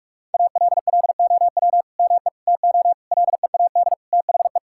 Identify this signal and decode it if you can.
Morse code, heard as a keyed tone, about 41 wpm, 700 Hz. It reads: ALLOW ME TO LEAD THE
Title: Sentences from Top 1000 Words - Review Part 5 - 45wpm